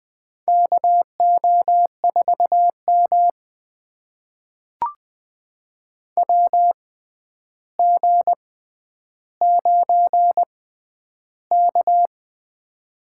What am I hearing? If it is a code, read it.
Morse code, heard as a keyed tone, about 21 wpm, spaced as for 10 wpm, 700 Hz. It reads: KO4M WG9K
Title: USA Call Signs - 6x Character Spacing - 20wpm